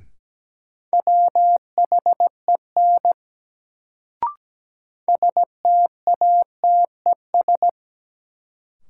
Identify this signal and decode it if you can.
Morse code, heard as a keyed tone, about 17 wpm, 700 Hz. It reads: WHEN STATES
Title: Sets of 2 Words - New Word Drill - 17wpm